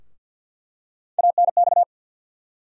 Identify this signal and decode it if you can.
Morse code, heard as a keyed tone, about 48 wpm, 700 Hz. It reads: ANX
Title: USA Call Sign Postfixes - 3 Letters 50wpm